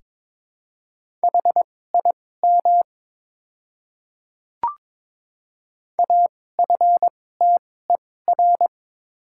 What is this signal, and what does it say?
Morse code, heard as a keyed tone, about 22 wpm, spaced as for 16 wpm, 700 Hz. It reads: HIM AFTER
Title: 2x Character Spacing - Top 200 Words - 22wpm